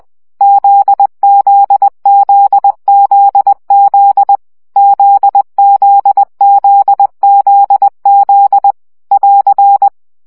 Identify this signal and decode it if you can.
Morse code, heard as a keyed tone, about 20 wpm, 800 Hz. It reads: ZZZZZ ZZZZZ <AR>
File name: CW gyakorlas z 17 baud